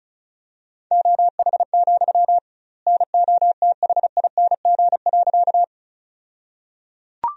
Morse code, heard as a keyed tone, about 34 wpm, 700 Hz. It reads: OH, NOTHING.
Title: 2x Word-Spacing - Sentences from Top 500 Words - 35wpm